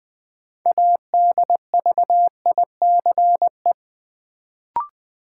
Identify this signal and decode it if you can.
Morse code, heard as a keyed tone, about 20 wpm, 700 Hz. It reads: ADVICE
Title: Sentences from Top 1100 Words - New Word Drill - 20wpm